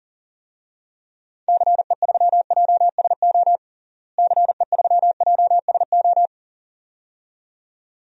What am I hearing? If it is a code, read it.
Morse code, heard as a keyed tone, about 40 wpm, 700 Hz. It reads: CE3JSO CE3JSO
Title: DXCC-100 Call Signs - 6 Characters - Repeated 2x - 40wpm